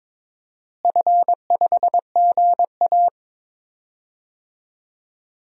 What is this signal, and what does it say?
Morse code, heard as a keyed tone, about 22 wpm, 700 Hz. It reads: F5GA
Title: DXCC-100 Call Signs - 4 Characters - Repeated 3x - 22wpm